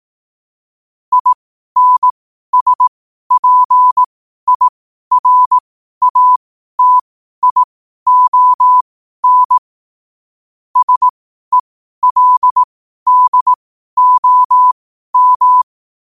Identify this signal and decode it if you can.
Morse code, heard as a keyed tone, about 18 wpm, spaced as for 12 wpm, 1000 Hz. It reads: INSPIRATION SELDOM
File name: qotdfiles / 12wpm